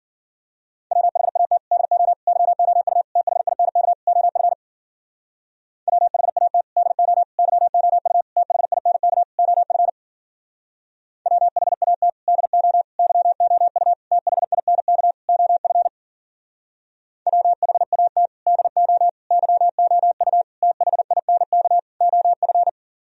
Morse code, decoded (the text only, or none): WHAT DO YOU THINK OF WHAT DO YOU THINK OF WHAT DO YOU THINK OF WHAT DO YOU THINK OF